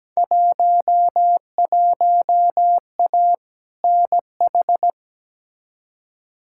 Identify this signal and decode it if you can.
Morse code, heard as a keyed tone, about 17 wpm, 700 Hz. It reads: 11A NH
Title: Field Day Exchanges Encore - 17wpm